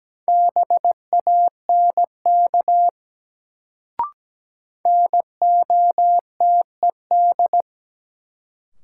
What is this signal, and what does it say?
Morse code, heard as a keyed tone, about 17 wpm, 700 Hz. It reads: BANK NOTED
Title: Sets of 5 Words - New Word Drill - 17wpm